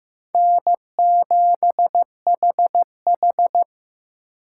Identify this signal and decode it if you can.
Morse code, heard as a keyed tone, about 18 wpm, 700 Hz. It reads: N7HH
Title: Speed Racing - USA 1x2 Call Signs 15wpm